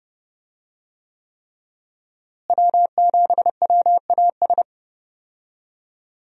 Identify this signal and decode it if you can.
Morse code, heard as a keyed tone, about 30 wpm, 700 Hz. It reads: W7WAS
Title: USA Call Signs - 5 Characters - Repeated 3x - 30wpm